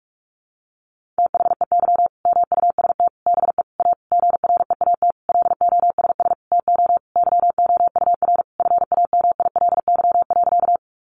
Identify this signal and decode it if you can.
Morse code, heard as a keyed tone, about 44 wpm, 700 Hz. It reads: THEY MUST BE A GREAT LOSS TO YOUR FAMILY.
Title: Sentences from Top 1400 Words 45wpm